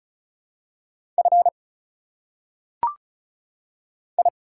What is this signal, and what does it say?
Morse code, heard as a keyed tone, about 36 wpm, 700 Hz. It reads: F I
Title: Single Letters - Phonetic Pronunciation 35wpm